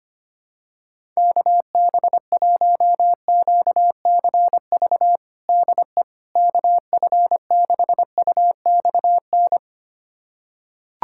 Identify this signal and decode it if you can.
Morse code, heard as a keyed tone, about 25 wpm, 700 Hz. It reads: KB1QCV DE KF6UXN
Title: USA Callsign DE Callsign - 6 Characters - 25wpm